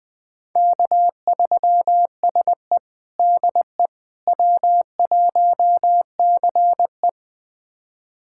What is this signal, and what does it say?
Morse code, heard as a keyed tone, about 20 wpm, 700 Hz. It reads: K3SE DE W1CE
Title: USA Callsign DE Callsign - 4 Characters - 20wpm